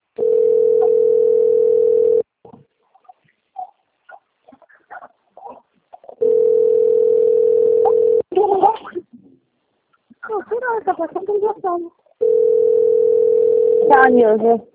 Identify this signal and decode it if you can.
Ringback tone (North America), 440 + 480 Hz, with speech.
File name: rg-1600-unknown- - - .14163